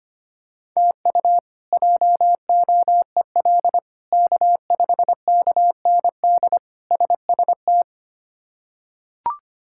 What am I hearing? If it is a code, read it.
Morse code, heard as a keyed tone, about 25 wpm, 700 Hz. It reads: TU JOEL K5KND SST
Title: K1USN SST QSO Practice - 25wpm